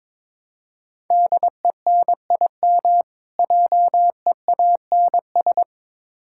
Morse code, heard as a keyed tone, about 22 wpm, 700 Hz. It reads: DENIM JEANS
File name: Difficult Words with Hints - 22wpm